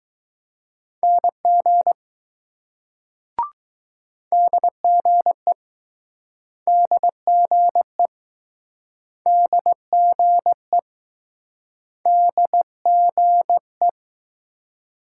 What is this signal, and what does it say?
Morse code, heard as a keyed tone, about 20 wpm, 700 Hz. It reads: NG DGE DGE DGE DGE